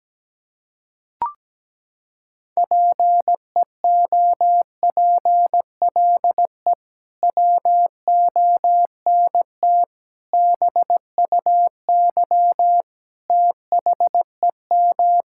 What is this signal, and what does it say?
Morse code, heard as a keyed tone, about 17 wpm, 700 Hz. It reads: PEOPLE WONT BUY THEM
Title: Sentences from Top 1000 Words - Review Part 6 - 17wpm